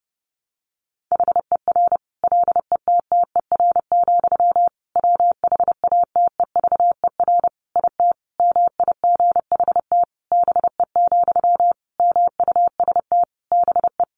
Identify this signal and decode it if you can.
Morse code, heard as a keyed tone, about 30 wpm, 700 Hz. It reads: HER LETTER, WHATEVER IT MIGHT BE, MUST BE
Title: Sentences from Top 1100 Words 30wpm